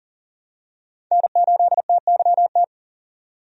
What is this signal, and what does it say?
Morse code, heard as a keyed tone, about 40 wpm, 700 Hz. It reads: N8TYT
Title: USA Call Signs - 5 Characters - 40wpm